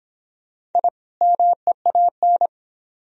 Morse code, heard as a keyed tone, about 24 wpm, 700 Hz. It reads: I MEAN